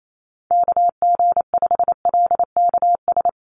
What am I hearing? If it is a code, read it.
Morse code, heard as a keyed tone, about 28 wpm, 700 Hz. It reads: KG5LKS
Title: USA Call Signs Repeated 3x 28wpm